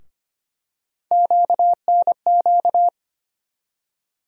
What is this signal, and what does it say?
Morse code, heard as a keyed tone, about 25 wpm, 700 Hz. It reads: QNQ
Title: USA Call Sign Postfixes - 3 Letters 25wpm